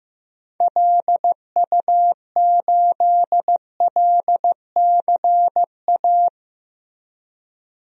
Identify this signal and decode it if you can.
Morse code, heard as a keyed tone, about 15 wpm, 700 Hz. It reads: LU8LCA
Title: DXCC-100 Call Signs - 6 Characters - Repeated 2x - 15wpm